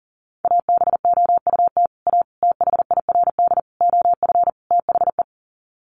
Morse code, heard as a keyed tone, about 40 wpm, 700 Hz. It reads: ABOUT A THIRD OF THE
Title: Sets of 5 Words Encore2 40wpm